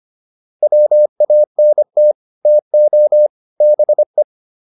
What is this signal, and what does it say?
Morse code, heard as a keyed tone, about 25 wpm, 600 Hz. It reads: WANT TO BE
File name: WD205 25